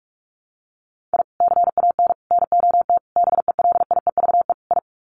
Morse code, heard as a keyed tone, about 44 wpm, 700 Hz. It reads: I CAN NOT BELIEVE I
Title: Sets of 5 Words Encore2 45wpm